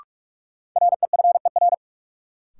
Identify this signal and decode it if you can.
Morse code, heard as a keyed tone, about 47 wpm, 700 Hz. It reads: REFER
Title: Sentences from Top 1000 Words - New Word Drill - 45wpm